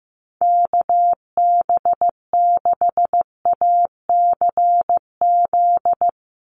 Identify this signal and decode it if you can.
Morse code, heard as a keyed tone, about 15 wpm, 700 Hz. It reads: KB6ACZ
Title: USA Call Signs 15wpm